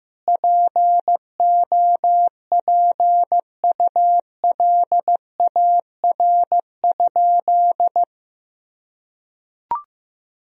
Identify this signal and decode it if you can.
Morse code, heard as a keyed tone, about 15 wpm, 700 Hz. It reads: POPULAR?
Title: Sentences from Top 1000 Words - Review Part 3 - 15wpm